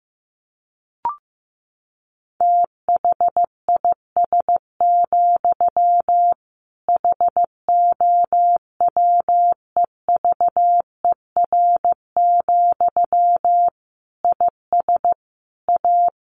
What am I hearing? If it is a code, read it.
Morse code, heard as a keyed tone, about 15 wpm, 700 Hz. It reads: THIS, HOWEVER, IS A
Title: Sentences from Top 1500 Words 15wpm